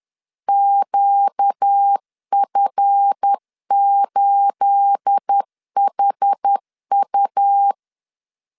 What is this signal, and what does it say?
Morse code, heard as a keyed tone, about 10 wpm, 800 Hz. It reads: QF8HU
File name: LEZIONI / 9 28 10wpm 2842